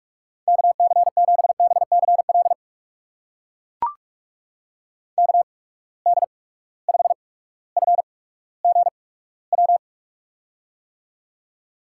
Morse code, heard as a keyed tone, about 44 wpm, spaced as for 19 wpm, 700 Hz. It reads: KC7BCL KD5FGW
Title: USA Call Signs - 6 Characters - 8x Character Spacing - 45wpm